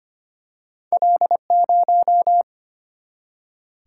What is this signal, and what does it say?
Morse code, heard as a keyed tone, about 25 wpm, 700 Hz. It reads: L0
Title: Single-Letter Single-Number 25wpm